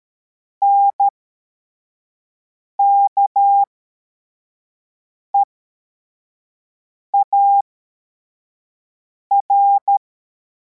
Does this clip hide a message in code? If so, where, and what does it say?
Yes: Morse code from 0.6 to 10.0 s, heard as a keyed tone, about 13 wpm, spaced as for 4 wpm, 800 Hz. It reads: NKEAR